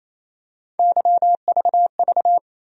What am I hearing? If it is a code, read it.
Morse code, heard as a keyed tone, about 27 wpm, 700 Hz. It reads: YVV
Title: USA Call Sign Postfixes - 3 Letters 28wpm